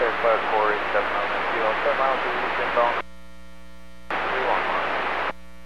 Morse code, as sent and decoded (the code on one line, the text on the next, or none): none